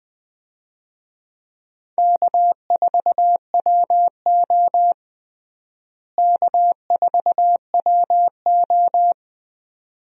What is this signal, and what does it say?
Morse code, heard as a keyed tone, about 20 wpm, 700 Hz. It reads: K4WO K4WO
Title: USA Call Signs - 4 Characters - Repeated 2x - 20wpm